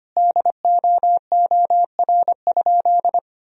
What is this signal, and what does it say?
Morse code, heard as a keyed tone, about 25 wpm, 700 Hz. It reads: DOOR?
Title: Sentences from Top 1000 Words - Review Part 4 - 25wpm